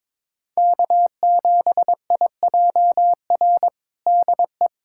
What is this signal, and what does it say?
Morse code, heard as a keyed tone, about 22 wpm, 700 Hz. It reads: K7IJR DE
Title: USA Callsign DE Callsign - 5 Characters - 22wpm